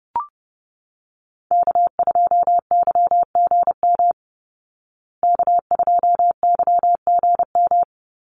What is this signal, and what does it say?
Morse code, heard as a keyed tone, about 30 wpm, 700 Hz. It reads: K2YGM K2YGM
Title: Call Signs Repeated 3x - 30wpm